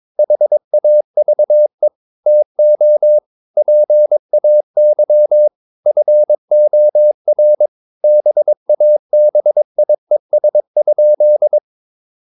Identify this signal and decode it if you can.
Morse code, heard as a keyed tone, about 22 wpm, 600 Hz. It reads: HAVE TO PAY FOR BABIES?
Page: Romanen Never af Ken Follet på engelsk.